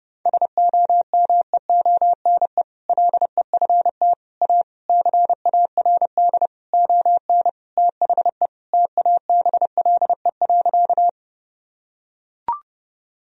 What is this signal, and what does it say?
Morse code, heard as a keyed tone, about 29 wpm, 700 Hz. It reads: SOMEONE LEFT A CARD ON THE TABLE.
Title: Sentences from Top 1000 Words - Review Part 4 - 30wpm